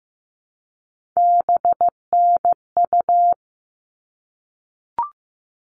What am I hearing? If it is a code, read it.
Morse code, heard as a keyed tone, about 15 wpm, 700 Hz. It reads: BNU
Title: All Cut Numbers - 3-Digit 15wpm